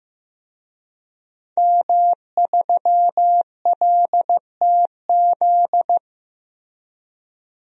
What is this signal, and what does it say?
Morse code, heard as a keyed tone, about 15 wpm, 700 Hz. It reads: M3LTZ
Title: DXCC-100 Call Signs - 5 Characters - Repeated 3x - 15wpm